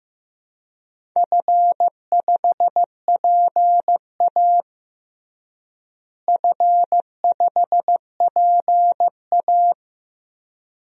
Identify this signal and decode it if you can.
Morse code, heard as a keyed tone, about 15 wpm, 700 Hz. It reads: F5PA F5PA
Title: DXCC-100 Call Signs - 4 Characters - Repeated 2x - 15wpm